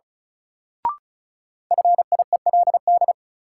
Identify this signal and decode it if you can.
Morse code, heard as a keyed tone, about 36 wpm, 700 Hz. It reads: FIELD
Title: Sentences from Top 1000 Words - New Word Drill - 35wpm